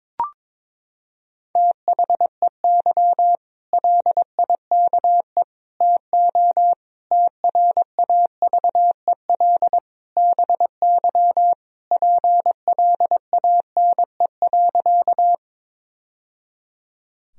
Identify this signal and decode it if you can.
Morse code, heard as a keyed tone, about 22 wpm, 700 Hz. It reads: THEY LIKE TO TRAVEL BY PLANE.
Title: Sentences from Top 1000 Words - Review Part 3 - 22wpm